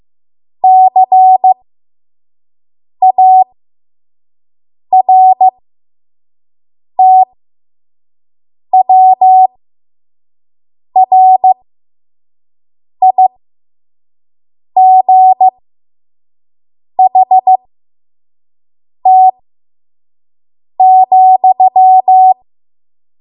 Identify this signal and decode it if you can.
Morse code, heard as a keyed tone, about 15 wpm, spaced as for 5 wpm, 750 Hz. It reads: CARTWRIGHT,
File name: Morse / 05WPM